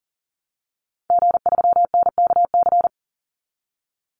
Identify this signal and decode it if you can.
Morse code, heard as a keyed tone, about 40 wpm, 700 Hz. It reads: G3NKC
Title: Call Signs Repeated 2x - 40wpm